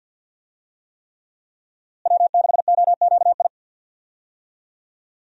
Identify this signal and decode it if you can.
Morse code, heard as a keyed tone, about 51 wpm, 700 Hz. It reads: W6OQI
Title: USA Call Signs - 5 Characters - Repeated 3x - 50wpm